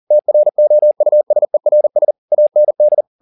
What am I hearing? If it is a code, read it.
Morse code, heard as a keyed tone, about 40 wpm, 600 Hz. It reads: TROUSERS AND